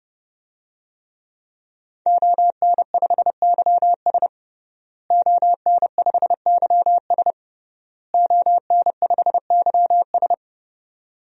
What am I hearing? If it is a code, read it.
Morse code, heard as a keyed tone, about 30 wpm, 700 Hz. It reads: ON5YS ON5YS ON5YS
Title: DXCC-100 Call Signs - 5 Characters - Repeated 3x - 30wpm